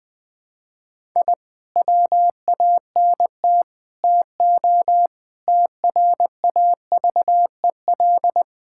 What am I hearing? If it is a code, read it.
Morse code, heard as a keyed tone, about 20 wpm, 700 Hz. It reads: I WANT TO TRAVEL
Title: Sentences with 5-Word Phrases - Spoken Beginning - 20wpm